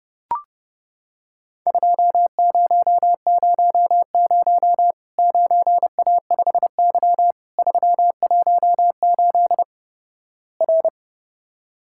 Morse code, heard as a keyed tone, about 32 wpm, 700 Hz. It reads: 2000 9A5Y 318 R
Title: Worked All Europe DX Contest - QTC Exchange - 30wpm